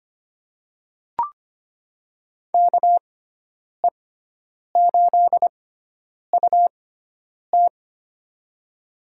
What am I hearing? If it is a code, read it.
Morse code, heard as a keyed tone, about 25 wpm, spaced as for 9 wpm, 700 Hz. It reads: KE8UT
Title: USA Call Signs - 6x Character Spacing - 25wpm